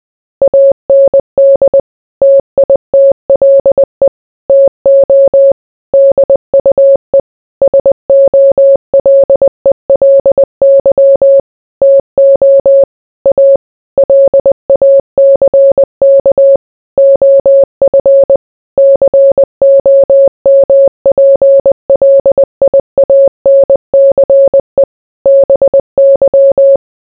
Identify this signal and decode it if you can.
Morse code, heard as a keyed tone, about 20 wpm, 550 Hz. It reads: AND TITLE TO DUE SOLELY TO A LACK OF COMPLIANCE BY